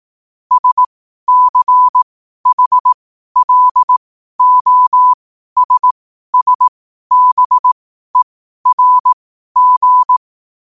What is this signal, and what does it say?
Morse code, heard as a keyed tone, about 18 wpm, spaced as for 12 wpm, 1000 Hz. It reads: SCHLOSSBERG
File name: qotdfiles / 12wpm